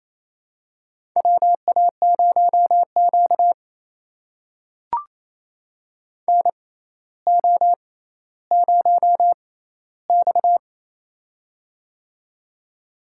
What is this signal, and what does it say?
Morse code, heard as a keyed tone, about 28 wpm, spaced as for 14 wpm, 700 Hz. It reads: WA0Q NO0X
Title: USA Call Signs - 4 Characters - 6x Character Spacing - 28wpm